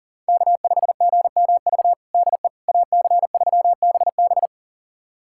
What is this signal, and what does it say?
Morse code, heard as a keyed tone, about 37 wpm, 700 Hz. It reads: K5GMV DE AC3BB
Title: USA Callsign DE Callsign - 5 Characters - 40wpm